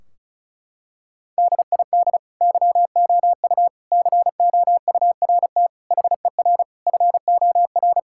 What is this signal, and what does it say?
Morse code, heard as a keyed tone, about 34 wpm, 700 Hz. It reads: DID YOU COURT HER FOR
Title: Sentences from Top 1600 Words 35wpm